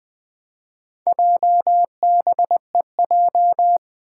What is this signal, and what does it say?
Morse code, heard as a keyed tone, about 20 wpm, 700 Hz. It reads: JBEJ
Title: Four Challenging Letter-Number Combinations 20wpm